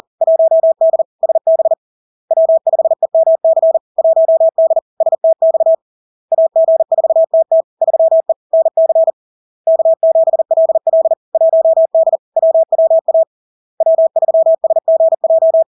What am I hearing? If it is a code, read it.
Morse code, heard as a keyed tone, about 40 wpm, 650 Hz. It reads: 1D SB W5EMC 1D STX AG4TT 3E NC K7LL 1D WWA W3SGJ
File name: LARK-FD-2022-Log-in-CW-40wpm-650-Hz